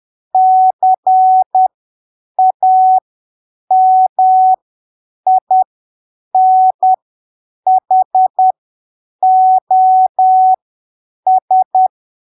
Morse code, heard as a keyed tone, about 10 wpm, spaced as for 7 wpm, 750 Hz. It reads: CAMINHOS
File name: arquivos / texto / texto 19 10ppm